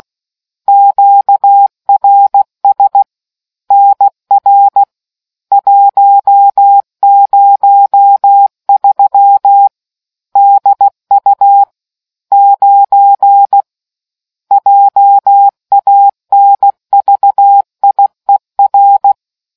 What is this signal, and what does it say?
Morse code, heard as a keyed tone, about 16 wpm, 800 Hz. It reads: QRS NR 103 DU 9 JANVIER